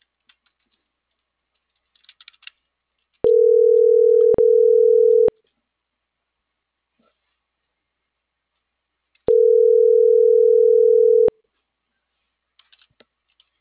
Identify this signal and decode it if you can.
Ringback tone (North America), 440 + 480 Hz.